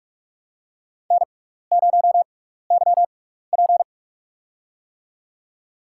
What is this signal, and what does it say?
Morse code, heard as a keyed tone, about 46 wpm, spaced as for 15 wpm, 700 Hz. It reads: N0YP
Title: USA Call Signs - 4 Characters - 6x Character Spacing - 45wpm